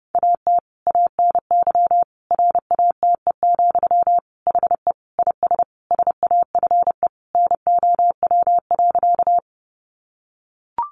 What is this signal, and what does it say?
Morse code, heard as a keyed tone, about 30 wpm, 700 Hz. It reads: AT ANY RATE, HE IS SAFE NOW.
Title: Sentences from Top 900 Words 30wpm